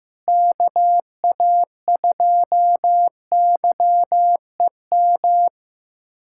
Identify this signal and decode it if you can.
Morse code, heard as a keyed tone, about 15 wpm, 700 Hz. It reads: KA2YEM
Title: USA Callsign DE Callsign - 6 Characters - 15wpm